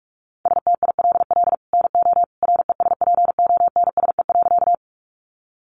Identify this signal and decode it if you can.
Morse code, heard as a keyed tone, about 45 wpm, 700 Hz. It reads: STILL NO RESPONSE.
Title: Sentences from Top 1400 Words 45wpm